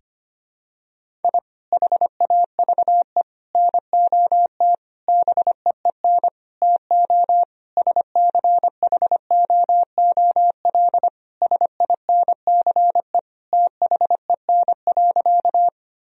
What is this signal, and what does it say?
Morse code, heard as a keyed tone, about 25 wpm, 700 Hz. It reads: I HAVE NOT BEEN TO SCHOOL SINCE THEN.
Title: Sentences from Top 1000 Words - Review Part 1 - 25wpm